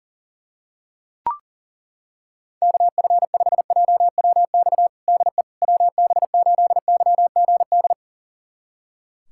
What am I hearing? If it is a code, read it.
Morse code, heard as a keyed tone, about 40 wpm, 700 Hz. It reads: KF5JWX DE WB8YGD
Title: USA Callsign DE Callsign 40wpm